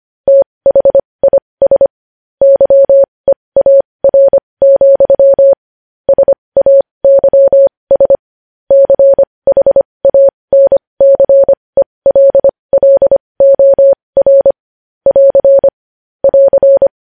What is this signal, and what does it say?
Morse code, heard as a keyed tone, about 25 wpm, spaced as for 20 wpm, 550 Hz. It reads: THIS YEAR, SAYS CHANCELLOR <AR> <AR>
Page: news-headlines-in-morse-code-at-20-wpm